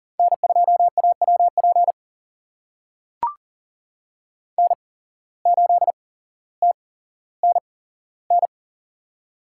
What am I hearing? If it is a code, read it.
Morse code, heard as a keyed tone, about 40 wpm, spaced as for 17 wpm, 700 Hz. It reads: N2AWP N8TNN
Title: USA Call Signs - 5 Characters - 8x Character Spacing - 40wpm